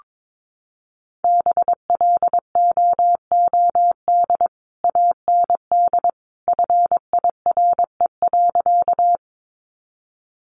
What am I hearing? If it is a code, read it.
Morse code, heard as a keyed tone, about 22 wpm, 700 Hz. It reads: BLOOD AND FIRE.
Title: Sentences from Top 1100 Words 22wpm